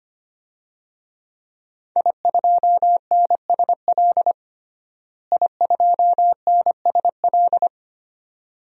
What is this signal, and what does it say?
Morse code, heard as a keyed tone, about 25 wpm, 700 Hz. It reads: I2NSL I2NSL
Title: DXCC-100 Call Signs - 5 Characters - Repeated 2x - 25wpm